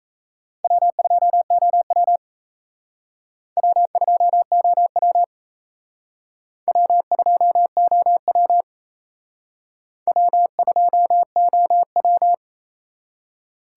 Morse code, heard as a keyed tone, about 35 wpm, 700 Hz. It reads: W2OW W2OW W2OW W2OW
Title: Speed Racing - USA 1x2 Call Signs 28wpm